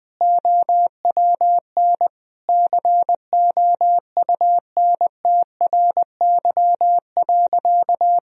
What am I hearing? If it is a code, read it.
Morse code, heard as a keyed tone, about 21 wpm, 700 Hz. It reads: OWN COUNTRY.
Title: Sentences from Top 1000 Words - Review Part 1 - 20wpm